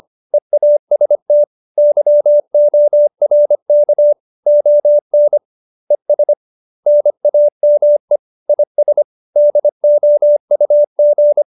Morse code, heard as a keyed tone, about 25 wpm, 590 Hz. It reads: EAST YORK ON ES NAME IS DOUG
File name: QSO303 25